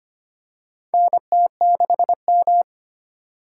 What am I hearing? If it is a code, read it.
Morse code, heard as a keyed tone, about 25 wpm, 700 Hz. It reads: NT6M
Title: USA Call Signs - 4 Characters - 3x Character Spacing - 25wpm